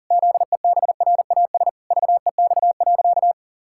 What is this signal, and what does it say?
Morse code, heard as a keyed tone, about 40 wpm, 700 Hz. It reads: ZEBRAS VEX.
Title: Pangram Challenge v2 - 40wpm